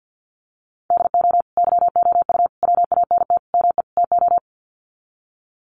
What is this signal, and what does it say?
Morse code, heard as a keyed tone, about 48 wpm, 700 Hz. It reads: DO YOU WANT ME TO
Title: Sets of 5 Words 50wpm